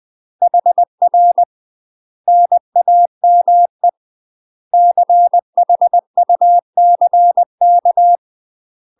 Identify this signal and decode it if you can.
Morse code, heard as a keyed tone, about 20 wpm, 700 Hz. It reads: HR NAME CHUCK
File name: UK MASTER QSOs TRAINING / 20WPM / UK MASTER QSOs 0043